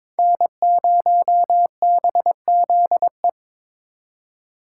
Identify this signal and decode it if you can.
Morse code, heard as a keyed tone, about 25 wpm, 700 Hz. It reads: N0BZE
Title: Speed Racing - USA 1x3 Call Signs 22wpm